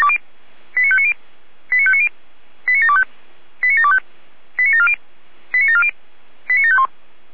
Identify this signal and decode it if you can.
Five-tone selective call (ZVEI), heard as quick runs of five tones, tones of about 70 ms. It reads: 89590 89590 89725 89725 89740 89740 89751